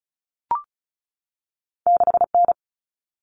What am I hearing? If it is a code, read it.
Morse code, heard as a keyed tone, about 36 wpm, 700 Hz. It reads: 6N
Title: Common Cut Numbers - 2-Digit 35wpm